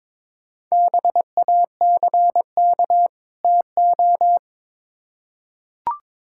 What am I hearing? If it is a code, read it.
Morse code, heard as a keyed tone, about 22 wpm, 700 Hz. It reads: BACK TO
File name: Build A Phrase - 2-Word Phrases - 22wpm